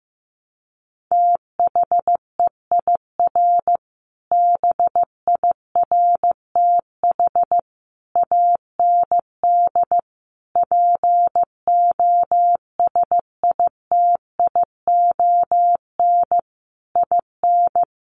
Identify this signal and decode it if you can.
Morse code, heard as a keyed tone, about 15 wpm, 700 Hz. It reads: THEIR BIRTH AND POSITION IN
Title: Sentences from Top 900 Words 15wpm